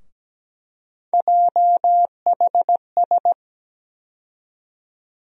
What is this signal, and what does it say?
Morse code, heard as a keyed tone, about 17 wpm, 700 Hz. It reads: JHS
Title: USA Call Sign Postfixes - 3 Letters 17wpm